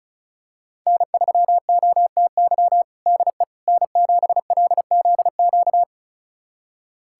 Morse code, heard as a keyed tone, about 36 wpm, 700 Hz. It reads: N3OTY DE N7LZQ
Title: USA Callsign DE Callsign - 5 Characters - 35wpm